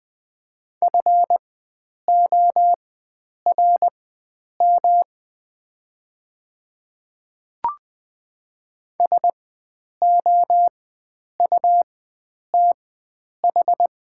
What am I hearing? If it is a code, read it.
Morse code, heard as a keyed tone, about 20 wpm, spaced as for 9 wpm, 700 Hz. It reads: FORM SOUTH
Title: 4x Character Spacing - Top 500 Words - 20wpm